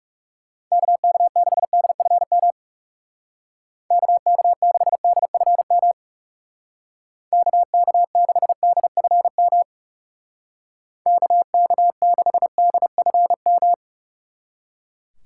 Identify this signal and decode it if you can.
Morse code, heard as a keyed tone, about 38 wpm, 700 Hz. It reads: KK6DFM KK6DFM KK6DFM KK6DFM